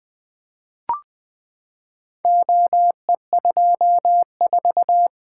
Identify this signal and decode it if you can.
Morse code, heard as a keyed tone, about 20 wpm, 700 Hz. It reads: OE24
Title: Four Letter-Number Combinations - 20wpm